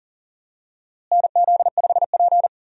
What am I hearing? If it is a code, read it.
Morse code, heard as a keyed tone, about 40 wpm, 700 Hz. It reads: NZ5P
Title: USA Call Signs - 4 Characters - Repeated 2x - 40wpm